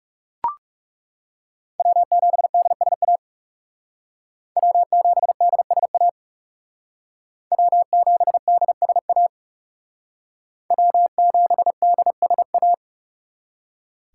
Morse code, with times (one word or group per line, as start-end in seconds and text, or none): 1.8-3.2 W7DSA
4.6-6.1 W7DSA
7.5-9.3 W7DSA
10.7-12.7 W7DSA